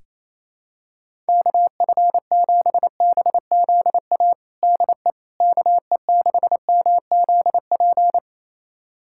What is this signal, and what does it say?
Morse code, heard as a keyed tone, about 28 wpm, 700 Hz. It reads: KF7BZA DE KE6MZP
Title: USA Callsign DE Callsign - 6 Characters - 28wpm